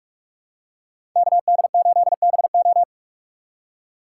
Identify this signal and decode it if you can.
Morse code, heard as a keyed tone, about 45 wpm, 700 Hz. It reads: KD8BO
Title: USA Call Signs - 5 Characters - 4x Character Spacing - 45wpm